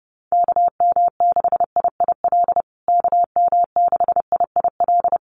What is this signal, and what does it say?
Morse code, heard as a keyed tone, about 30 wpm, 700 Hz. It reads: KM6IIL KM6IIL
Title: USA Call Signs Repeated 2x 30wpm